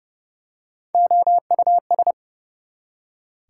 Morse code, heard as a keyed tone, about 30 wpm, 700 Hz. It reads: OUS
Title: Prefixes and Suffixes 30wpm